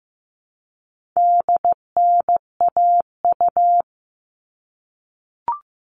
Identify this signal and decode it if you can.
Morse code, heard as a keyed tone, about 15 wpm, 700 Hz. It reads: DNAU
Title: All Cut Numbers - 4-Digit 15wpm